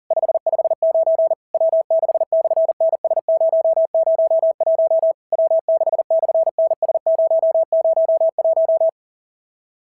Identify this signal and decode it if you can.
Morse code, heard as a keyed tone, about 39 wpm, 660 Hz. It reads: 559 W6/NS001 W6/NS001
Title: SOTA QSO Practice 40wpm